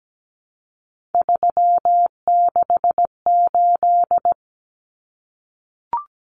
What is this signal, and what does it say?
Morse code, heard as a keyed tone, about 17 wpm, 700 Hz. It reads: 368
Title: Numbers - 3-Digit 17wpm